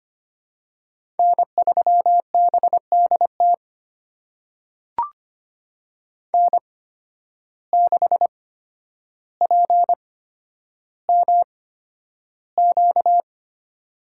Morse code, heard as a keyed tone, about 25 wpm, spaced as for 11 wpm, 700 Hz. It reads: N3BDT N6PMQ
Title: USA Call Signs - 5 Characters - 8x Character Spacing - 25wpm